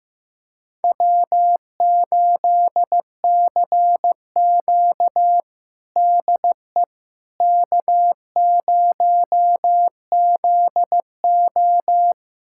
Morse code, heard as a keyed tone, about 16 wpm, 700 Hz. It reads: W8CQ DE K0ZO
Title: USA Callsign DE Callsign - 4 Characters - 15wpm